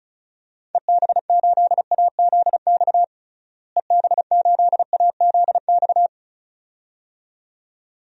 Morse code, heard as a keyed tone, about 36 wpm, 700 Hz. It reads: EB8AZX EB8AZX
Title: DXCC-100 Call Signs - 6 Characters - Repeated 2x - 35wpm